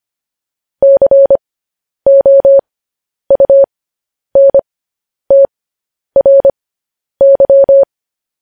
Morse code, heard as a keyed tone, about 25 wpm, spaced as for 10 wpm, 550 Hz. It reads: COUNTRY